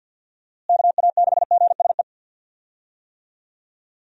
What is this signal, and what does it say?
Morse code, heard as a keyed tone, about 49 wpm, 700 Hz. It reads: KA6GSE
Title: USA Call Signs - 6 Characters - Repeated 3x - 50wpm